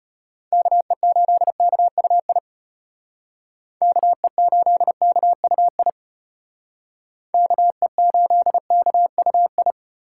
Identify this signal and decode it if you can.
Morse code, heard as a keyed tone, about 30 wpm, 700 Hz. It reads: KE8KUI KE8KUI KE8KUI
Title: Speed Racing - USA 2x3 Call Signs 25wpm